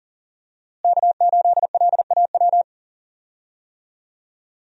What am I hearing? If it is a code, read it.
Morse code, heard as a keyed tone, about 40 wpm, 700 Hz. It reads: K8LAW
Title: USA Call Signs - 5 Characters - Repeated 3x - 40wpm